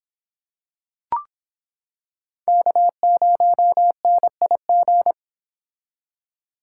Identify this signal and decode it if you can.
Morse code, heard as a keyed tone, about 25 wpm, 700 Hz. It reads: K0NIG